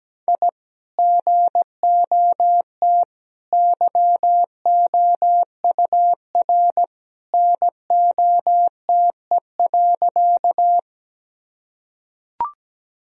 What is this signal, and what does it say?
Morse code, heard as a keyed tone, about 17 wpm, 700 Hz. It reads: I GOT YOUR NOTE.
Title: Sentences from Top 1000 Words - Review Part 2 - 17wpm